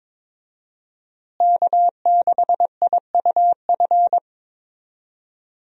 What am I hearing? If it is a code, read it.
Morse code, heard as a keyed tone, about 21 wpm, 700 Hz. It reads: K6IUF
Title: USA Call Signs - 5 Characters - Repeated 2x - 22wpm